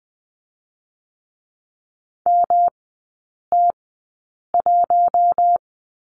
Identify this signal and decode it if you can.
Morse code, heard as a keyed tone, about 20 wpm, spaced as for 8 wpm, 700 Hz. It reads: MT1
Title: 121 Lesson 20e - Number 3 - Three Characters 20wpm